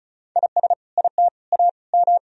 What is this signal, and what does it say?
Morse code, heard as a keyed tone, about 36 wpm, 700 Hz. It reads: IS IT A M
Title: Sentences from Top 1000 Words - Review Part 4 - 35wpm